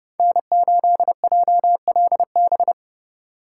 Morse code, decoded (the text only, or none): N8JLB